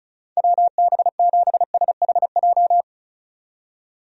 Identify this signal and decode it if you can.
Morse code, heard as a keyed tone, about 36 wpm, 700 Hz. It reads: WB7SHJ
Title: USA Call Signs - 6 Characters - 35wpm